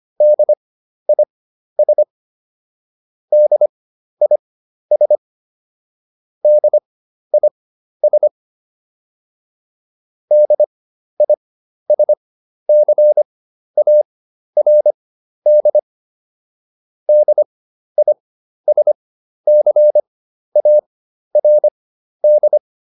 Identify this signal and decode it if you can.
Morse code, heard as a keyed tone, about 25 wpm, spaced as for 12 wpm, 600 Hz. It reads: DIS DIS DIS DISCARD DISCARD